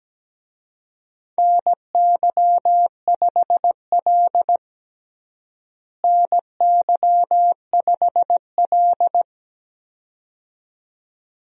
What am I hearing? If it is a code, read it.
Morse code, heard as a keyed tone, about 17 wpm, 700 Hz. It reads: NY5L NY5L
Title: USA Call Signs - 4 Characters - Repeated 2x - 17wpm